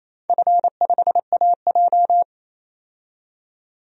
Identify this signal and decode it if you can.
Morse code, heard as a keyed tone, about 28 wpm, 700 Hz. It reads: F5AJ
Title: DXCC-100 Call Signs - 4 Characters - 28wpm